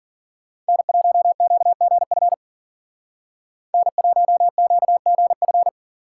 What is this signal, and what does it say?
Morse code, heard as a keyed tone, about 52 wpm, 700 Hz. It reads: N1QGF N1QGF